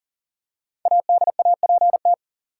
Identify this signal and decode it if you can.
Morse code, heard as a keyed tone, about 40 wpm, 700 Hz. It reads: ADAPT
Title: Five Letter Words - 40wpm